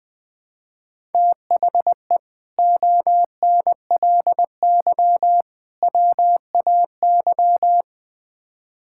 Morse code, heard as a keyed tone, about 20 wpm, 700 Hz. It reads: THE ONLY WAY
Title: Build A Phrase - 4-Word Phrases - 20wpm